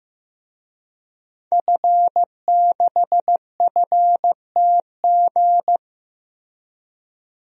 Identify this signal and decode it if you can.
Morse code, heard as a keyed tone, about 15 wpm, 700 Hz. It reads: F6FTG
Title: DXCC-100 Call Signs - 5 Characters - Repeated 2x - 15wpm